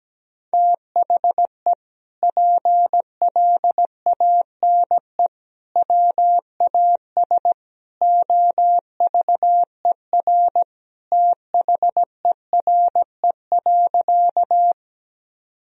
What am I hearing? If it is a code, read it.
Morse code, heard as a keyed tone, about 17 wpm, 700 Hz. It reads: THE PLANE WAS OVER THERE.
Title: Sentences from Top 1000 Words - Review Part 3 - 17wpm